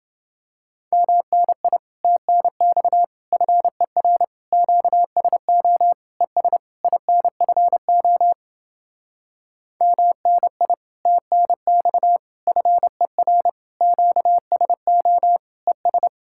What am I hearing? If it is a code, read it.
Morse code, heard as a keyed tone, about 29 wpm, 700 Hz. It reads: MNI TNX FER QSO ES INFO MNI TNX FER QSO ES